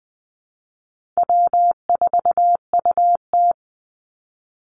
Morse code, heard as a keyed tone, about 20 wpm, 700 Hz. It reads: W4UT
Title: Call Signs - 20wpm